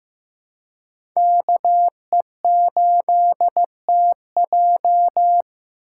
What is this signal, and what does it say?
Morse code, heard as a keyed tone, about 15 wpm, 700 Hz. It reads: KE8TJ
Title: USA Call Signs - 5 Characters - 3x Character Spacing - 15wpm